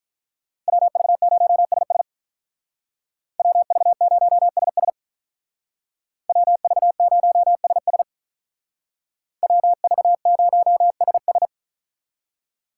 Morse code, heard as a keyed tone, about 43 wpm, 700 Hz. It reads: WV0SS WV0SS WV0SS WV0SS
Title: Speed Racing - USA 2x2 Call Signs 35wpm